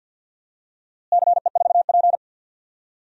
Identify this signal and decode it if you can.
Morse code, heard as a keyed tone, about 52 wpm, 700 Hz. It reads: CE4P
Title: DXCC-100 Call Signs - 4 Characters - Repeated 3x - 50wpm